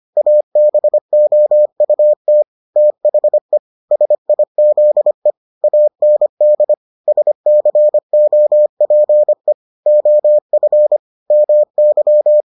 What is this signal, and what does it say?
Morse code, heard as a keyed tone, about 25 wpm, 600 Hz. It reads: ABOUT THE SIZE AND SCOPE OF MY